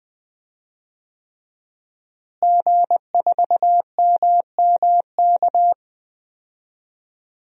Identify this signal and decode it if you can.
Morse code, heard as a keyed tone, about 20 wpm, 700 Hz. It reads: G4MMK
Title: DXCC-100 Call Signs - 5 Characters - Repeated 3x - 20wpm